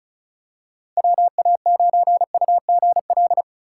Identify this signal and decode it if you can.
Morse code, heard as a keyed tone, about 36 wpm, 700 Hz. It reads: WA9UGL
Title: USA Call Signs - 2x Character Spacing - 35wpm